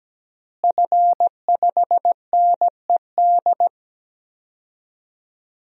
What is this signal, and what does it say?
Morse code, heard as a keyed tone, about 17 wpm, 700 Hz. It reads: F5NED
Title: DXCC-100 Call Signs - 5 Characters - Repeated 3x - 17wpm